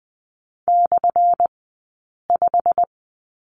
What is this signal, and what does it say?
Morse code, heard as a keyed tone, about 20 wpm, spaced as for 8 wpm, 700 Hz. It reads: /5
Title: 183 Lesson 30d - Letter b - Two Characters 20wpm